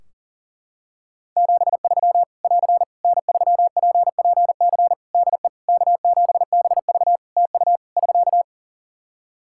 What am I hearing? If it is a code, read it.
Morse code, heard as a keyed tone, about 40 wpm, 700 Hz. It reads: 73 <AR> N3PPC DE K7BV TU <SK>
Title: Practice QSO - Exchange 4 - Ending - 40wpm